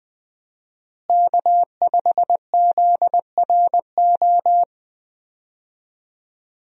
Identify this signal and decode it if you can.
Morse code, heard as a keyed tone, about 20 wpm, 700 Hz. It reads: K5ZRO
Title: USA Call Signs - 5 Characters - Repeated 3x - 20wpm